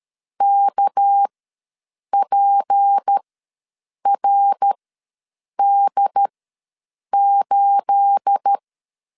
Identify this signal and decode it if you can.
Morse code, heard as a keyed tone, about 13 wpm, spaced as for 7 wpm, 800 Hz. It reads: KPRD8
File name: LEZIONI / 9 28 ws 12wpm 3354